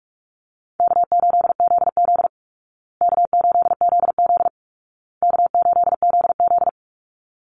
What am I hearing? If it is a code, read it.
Morse code, heard as a keyed tone, about 44 wpm, 700 Hz. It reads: K8ZZ K8ZZ K8ZZ
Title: Easy Call Signs Repeated 3x - 45wpm